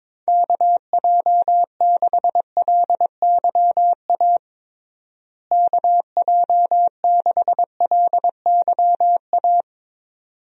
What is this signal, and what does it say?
Morse code, heard as a keyed tone, about 22 wpm, 700 Hz. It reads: KJ6LYA KJ6LYA
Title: USA Call Signs - 6 Characters - Repeated 2x - 22wpm